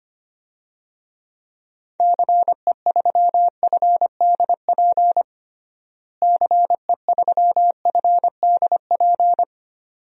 Morse code, heard as a keyed tone, about 25 wpm, 700 Hz. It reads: CE3FDP CE3FDP
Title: DXCC-100 Call Signs - 6 Characters - Repeated 2x - 25wpm